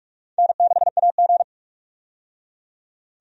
Morse code, heard as a keyed tone, about 46 wpm, 700 Hz. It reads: N6AG
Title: USA Call Signs - 4 Characters - 45wpm